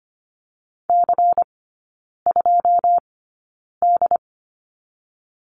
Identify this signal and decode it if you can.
Morse code, heard as a keyed tone, about 25 wpm, spaced as for 9 wpm, 700 Hz. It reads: C2D
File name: course / 25:10wpm / 093 Lesson 16e - Letter c - Three Characters 25wpm